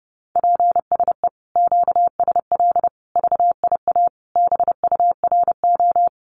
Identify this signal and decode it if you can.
Morse code, heard as a keyed tone, about 30 wpm, 700 Hz. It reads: PSE QSL VIA BURO
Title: Common QSO Phrases Encore 30wpm